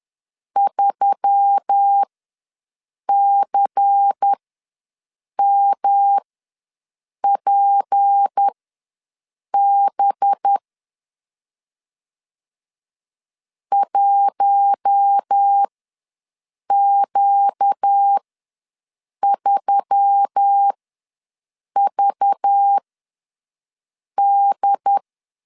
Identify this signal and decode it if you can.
Morse code, heard as a keyed tone, about 10 wpm, spaced as for 6 wpm, 800 Hz. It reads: 3CMPB 1Q3VD
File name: LEZIONI / 6 PKX0 / 6 P ws 10wpm 3751